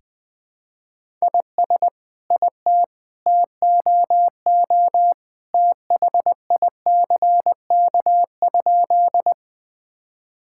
Morse code, heard as a keyed tone, about 21 wpm, 700 Hz. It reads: IS IT TOO THICK?
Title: Sentences from Top 1600 Words 20wpm